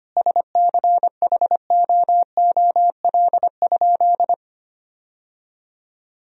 Morse code, heard as a keyed tone, about 25 wpm, 700 Hz. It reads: SCHOOL?
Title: Sentences from Top 1000 Words - Review Part 1 - 25wpm